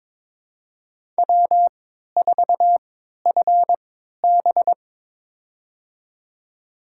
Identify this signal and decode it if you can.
Morse code, heard as a keyed tone, about 22 wpm, spaced as for 13 wpm, 700 Hz. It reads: W4FB
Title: USA Call Signs - 3x Character Spacing - 22wpm